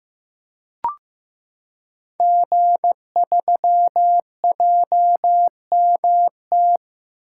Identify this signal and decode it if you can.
Morse code, heard as a keyed tone, about 15 wpm, 700 Hz. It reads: G3JMT
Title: Code Groups - 5 Letter:Numbers - 15wpm